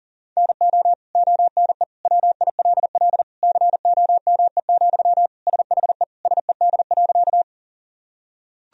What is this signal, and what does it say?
Morse code, heard as a keyed tone, about 37 wpm, 700 Hz. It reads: NO ONE WILL COME, SHE SED.
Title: Sentences from Top 100 Words and Abbreviations 40wpm